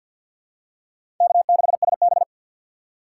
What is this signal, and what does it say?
Morse code, heard as a keyed tone, about 48 wpm, 700 Hz. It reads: K6SB
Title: USA Call Signs - 3x Character Spacing - 50wpm